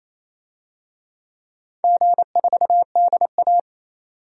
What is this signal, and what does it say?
Morse code, heard as a keyed tone, about 28 wpm, 700 Hz. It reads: G4DA
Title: DXCC-100 Call Signs - 4 Characters - 28wpm